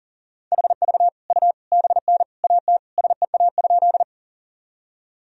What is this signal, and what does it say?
Morse code, heard as a keyed tone, about 40 wpm, 700 Hz. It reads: HV U BN AT SEA?
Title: Sentences from Top 300 Words and Abbreviations 40wpm